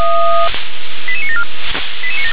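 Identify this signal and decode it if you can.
Five-tone selective call (ZVEI), heard as quick runs of five tones, tones of about 70 ms. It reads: 88974
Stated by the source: ZVEI five-tone call received by SDR.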